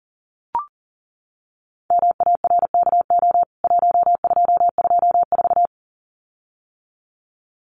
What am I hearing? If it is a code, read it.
Morse code, heard as a keyed tone, about 40 wpm, 700 Hz. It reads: MARKO 1224
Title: CWT Exchange 40wpm